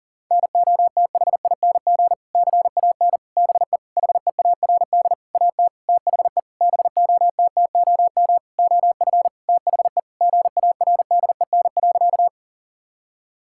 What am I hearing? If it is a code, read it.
Morse code, heard as a keyed tone, about 40 wpm, 700 Hz. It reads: NOTHING CAN BE HEARD AT THE BOTTOM OF THE GARDEN.
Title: Sentences from Top 1000 Words - Review Part 6 - 40wpm